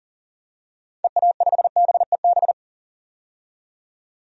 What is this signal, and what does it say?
Morse code, heard as a keyed tone, about 40 wpm, 700 Hz. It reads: EA5BEB
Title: DXCC-100 Call Signs - 6 Characters - 40wpm